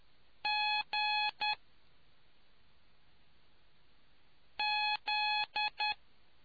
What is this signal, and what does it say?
Morse code, heard as a keyed tone, about 10 wpm, spaced as for 3 wpm, 840 Hz. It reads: GZ